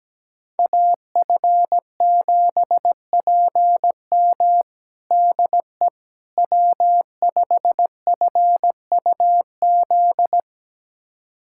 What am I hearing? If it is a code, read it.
Morse code, heard as a keyed tone, about 17 wpm, 700 Hz. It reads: AF7PM DE W5FUZ
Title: USA Callsign DE Callsign - 5 Characters - 17wpm